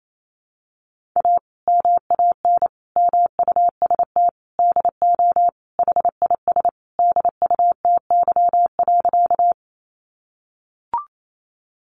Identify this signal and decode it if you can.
Morse code, heard as a keyed tone, about 28 wpm, 700 Hz. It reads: A MAN MUST DO HIS DUTY.
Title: Sentences from Top 1500 Words 28wpm